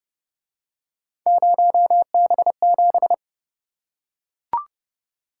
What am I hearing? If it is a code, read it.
Morse code, heard as a keyed tone, about 27 wpm, 700 Hz. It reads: 0B7